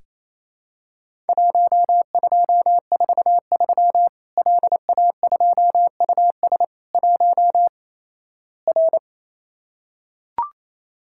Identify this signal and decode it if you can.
Morse code, heard as a keyed tone, about 28 wpm, 700 Hz. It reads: 1243 LA2US 1 R
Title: Worked All Europe DX Contest - QTC Exchange - 28wpm